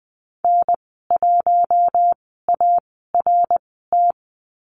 Morse code, heard as a keyed tone, about 20 wpm, spaced as for 15 wpm, 700 Hz. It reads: N1ART
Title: 055 Lesson 10g - Letter r - Call Signs 20wpm